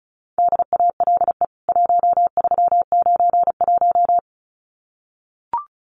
Morse code, heard as a keyed tone, about 34 wpm, 700 Hz. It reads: DALE 1391
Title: CWT Exchange 35wpm